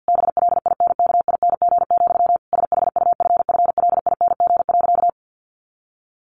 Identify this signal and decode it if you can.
Morse code, heard as a keyed tone, about 49 wpm, 700 Hz. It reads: BLINKING, SHUFFLING.
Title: A Sound Of Thunder 50wpm